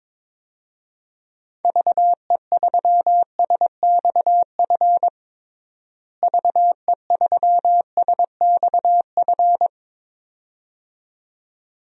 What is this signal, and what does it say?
Morse code, heard as a keyed tone, about 22 wpm, 700 Hz. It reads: VE3SXF VE3SXF
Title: DXCC-100 Call Signs - 6 Characters - Repeated 2x - 22wpm